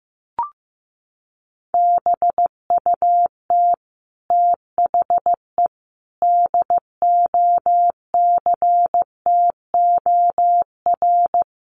Morse code, heard as a keyed tone, about 15 wpm, 700 Hz. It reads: BUT THE DOCTOR
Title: Sentences from Top 1500 Words 15wpm